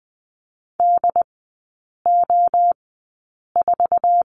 Morse code, heard as a keyed tone, about 20 wpm, spaced as for 8 wpm, 700 Hz. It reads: DO4
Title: 133 Lesson 22e - Question Mark - Three Characters 20wpm